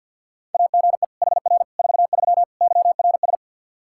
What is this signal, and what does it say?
Morse code, heard as a keyed tone, about 53 wpm, 700 Hz. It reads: AGE HR 43 YRS
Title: Practice QSO - Exchange 3 - More Info - 50wpm